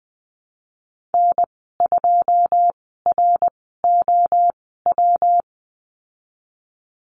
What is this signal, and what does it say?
Morse code, heard as a keyed tone, about 20 wpm, spaced as for 15 wpm, 700 Hz. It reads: N2ROW
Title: 128 Lesson 21g - Number 6 - Call Signs 20wpm